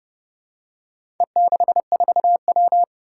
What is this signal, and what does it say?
Morse code, heard as a keyed tone, about 30 wpm, 700 Hz. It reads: E64W
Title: Four Challenging Letter-Number Combinations 30wpm